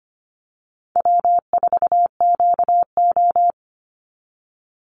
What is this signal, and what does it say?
Morse code, heard as a keyed tone, about 25 wpm, 700 Hz. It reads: W4QO
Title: Easy Call Signs Repeated 3x - 25wpm